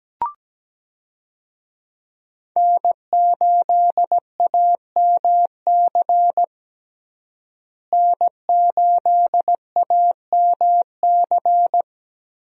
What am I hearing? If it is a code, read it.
Morse code, heard as a keyed tone, about 17 wpm, 700 Hz. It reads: N8AMC N8AMC
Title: USA Call Signs - 5 Characters - Repeated 3x - 17wpm